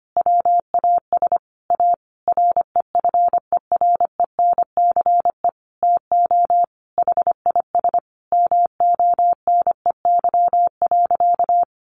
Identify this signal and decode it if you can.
Morse code, heard as a keyed tone, about 25 wpm, 700 Hz. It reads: WAS A REFERENCE TO HIS MONEY.
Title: Sentences from Top 1200 Words 25wpm